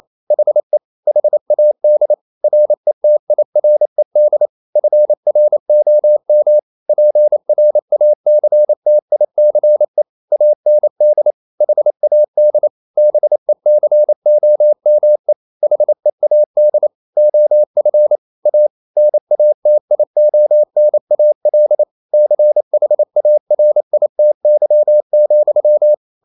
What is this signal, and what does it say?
Morse code, heard as a keyed tone, about 28 wpm, 600 Hz. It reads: HE HAD RETIRED FROM PRACTICE AND HAD BECOME HEAD OF A NATIONAL CHARITY,
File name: Never 0027wpm